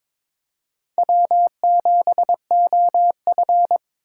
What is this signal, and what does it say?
Morse code, heard as a keyed tone, about 22 wpm, 700 Hz. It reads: W7OF
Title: USA Call Signs - 2x Character Spacing - 22wpm